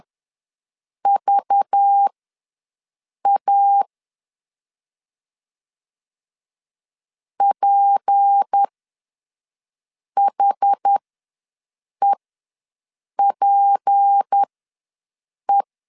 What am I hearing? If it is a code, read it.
Morse code, heard as a keyed tone, about 10 wpm, spaced as for 5 wpm, 800 Hz. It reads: VA PHEPE
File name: LEZIONI / 6 PKX0 / 6 P ws 10wpm 3751